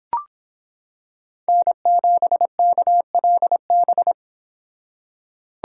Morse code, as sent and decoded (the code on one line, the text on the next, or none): -. --... -.- .-.. -...
N7KLB